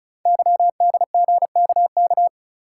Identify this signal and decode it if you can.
Morse code, heard as a keyed tone, about 35 wpm, 690 Hz. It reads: YDGKK
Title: Code Groups - 5 Letters - 35wpm